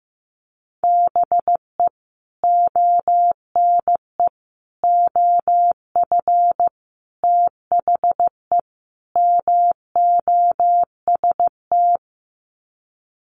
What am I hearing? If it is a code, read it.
Morse code, heard as a keyed tone, about 15 wpm, 700 Hz. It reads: BE ONE OF THE MOST
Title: Sets of 5 Words Encore 15wpm